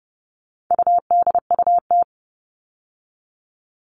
Morse code, heard as a keyed tone, about 30 wpm, 700 Hz. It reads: UDUT
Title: All Cut Numbers - 4-Digit 30wpm